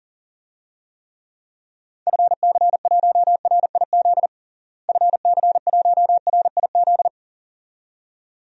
Morse code, heard as a keyed tone, about 40 wpm, 700 Hz. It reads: FC1RIZ FC1RIZ
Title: DXCC-100 Call Signs - 6 Characters - Repeated 2x - 40wpm